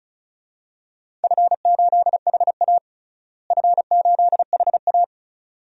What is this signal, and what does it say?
Morse code, heard as a keyed tone, about 36 wpm, 700 Hz. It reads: F8HA F8HA
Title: DXCC-100 Call Signs - 4 Characters - Repeated 2x - 35wpm